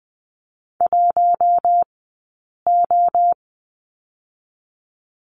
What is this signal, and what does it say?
Morse code, heard as a keyed tone, about 20 wpm, 700 Hz. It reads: 1 O
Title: 066 Lesson 12d - Letter d - Two Characters 20wpm